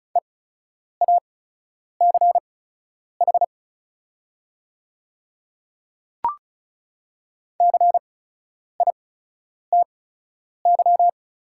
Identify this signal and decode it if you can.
Morse code, heard as a keyed tone, about 36 wpm, spaced as for 10 wpm, 700 Hz. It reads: EACH CITY
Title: 8x Character Spacing - Top 300 Words - 35wpm